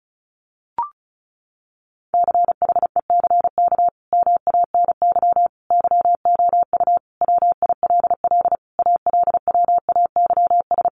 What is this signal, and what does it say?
Morse code, heard as a keyed tone, about 34 wpm, 700 Hz. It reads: CHECK MANY YOU WILL ALWAYS
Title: Sentences from Top 1500 Words 35wpm